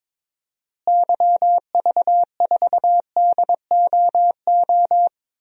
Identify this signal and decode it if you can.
Morse code, heard as a keyed tone, about 22 wpm, 700 Hz. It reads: YV4DOO
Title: DXCC-100 Call Signs - 6 Characters - Repeated 3x - 22wpm